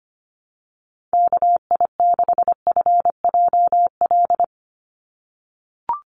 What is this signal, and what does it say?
Morse code, heard as a keyed tone, about 25 wpm, 700 Hz. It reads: KI6FJL
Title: USA Call Signs 25wpm